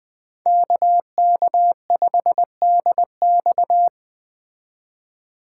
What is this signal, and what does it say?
Morse code, heard as a keyed tone, about 20 wpm, 700 Hz. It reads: KK5DX